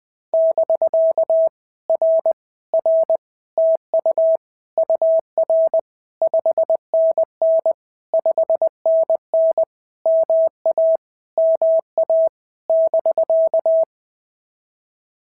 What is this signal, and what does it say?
Morse code, heard as a keyed tone, about 20 wpm, 650 Hz. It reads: <BK> R R TU UR 5NN 5NN MA MA <BK>
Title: POTA QSO Practice - 20wpm